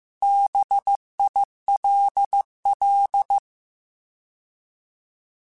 Morse code, heard as a keyed tone, about 15 wpm, 800 Hz. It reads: BILL
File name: Alphabetical / Alphabetical mp3s / Bs @13